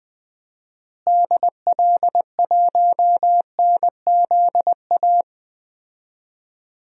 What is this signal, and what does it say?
Morse code, heard as a keyed tone, about 20 wpm, 700 Hz. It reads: DL1NZA
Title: DXCC-100 Call Signs - 6 Characters - Repeated 3x - 20wpm